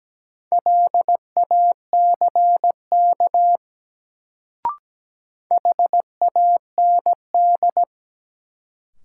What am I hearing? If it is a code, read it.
Morse code, heard as a keyed tone, about 17 wpm, 700 Hz. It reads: LACK HAND
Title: Sets of 3 Words - New Word Drill - 17wpm